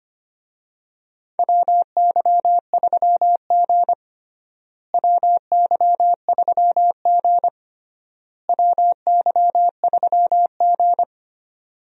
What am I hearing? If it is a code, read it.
Morse code, heard as a keyed tone, about 25 wpm, 700 Hz. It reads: WY3G WY3G WY3G
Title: USA Call Signs - 4 Characters - Repeated 3x - 25wpm